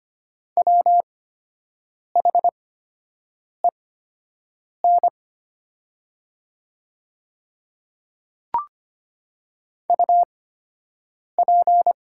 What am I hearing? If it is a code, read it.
Morse code, heard as a keyed tone, about 25 wpm, spaced as for 7 wpm, 700 Hz. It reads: WHEN UP
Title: 8x Character Spacing - Top 100 Words - 25wpm